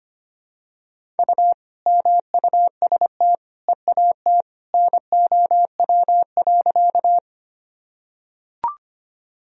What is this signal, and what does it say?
Morse code, heard as a keyed tone, about 25 wpm, 700 Hz. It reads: U MUST EAT NOW.
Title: Sentences from Top 300 Words and Abbreviations 25wpm